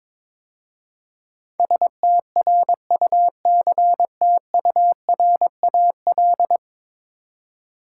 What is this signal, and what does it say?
Morse code, heard as a keyed tone, about 22 wpm, 700 Hz. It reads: STRUCTURAL
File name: practice / 22wpm / 10 Letter Words - First 3 Letters Given - 22wpm